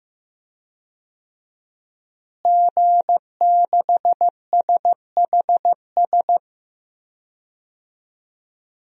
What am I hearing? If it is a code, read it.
Morse code, heard as a keyed tone, about 15 wpm, 700 Hz. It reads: G6SHS
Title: DXCC-100 Call Signs - 5 Characters - Repeated 3x - 15wpm